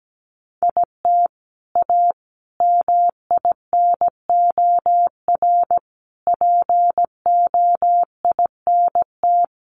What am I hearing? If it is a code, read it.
Morse code, heard as a keyed tone, about 17 wpm, 700 Hz. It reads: IT A MINOR POINT
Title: Sentences from Top 1500 Words 17wpm